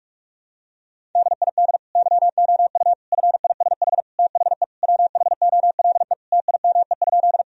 Morse code, heard as a keyed tone, about 41 wpm, 700 Hz. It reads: DID YOU FISH THE WHOLE TIME?
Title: Sentences from Top 1000 Words - Review Part 1 - 45wpm